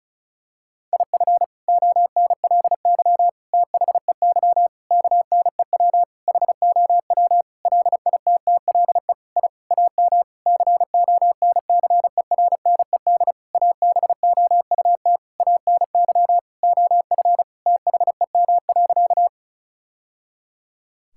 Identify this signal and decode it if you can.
Morse code, heard as a keyed tone, about 36 wpm, 700 Hz. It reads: IF ONLY THEY KNEW HOW LITTLE I AM CONCERNED ABOUT ANY OF THEM.
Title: Sentences from Top 1000 Words - Review Part 6 - 35wpm